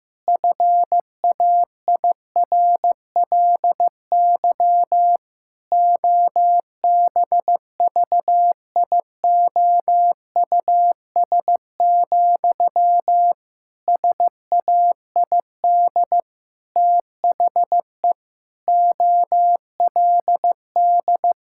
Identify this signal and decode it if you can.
Morse code, heard as a keyed tone, about 15 wpm, 700 Hz. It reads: FAIRLY OBVIOUS, SAID THE OLD
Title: Sentences from Top 1600 Words 15wpm